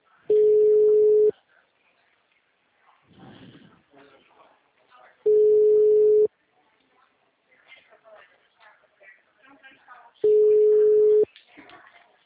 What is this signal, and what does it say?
Ringback tone (Europe, 425 Hz), with speech.
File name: out- -2025